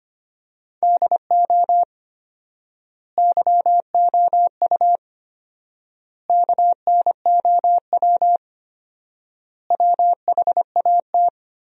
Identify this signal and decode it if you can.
Morse code, heard as a keyed tone, about 25 wpm, 700 Hz. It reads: DO YOU KNOW WHAT
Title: 4x Word-Spacing - Sentences from Top 100 Words - 25wpm